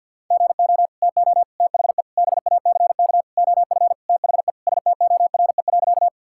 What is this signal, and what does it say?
Morse code, heard as a keyed tone, about 46 wpm, 700 Hz. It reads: GO TO THE BACK OF THE STORE.
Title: Sentences from Top 1000 Words - Review Part 5 - 50wpm